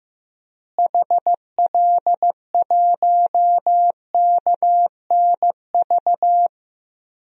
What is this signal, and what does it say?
Morse code, heard as a keyed tone, about 15 wpm, 700 Hz. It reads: HL1KNV
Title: DXCC-100 Call Signs - 6 Characters - Repeated 3x - 15wpm